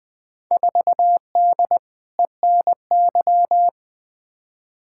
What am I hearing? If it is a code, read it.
Morse code, heard as a keyed tone, about 20 wpm, 700 Hz. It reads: 4D ENY
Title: Field Day Exchanges Encore - 20wpm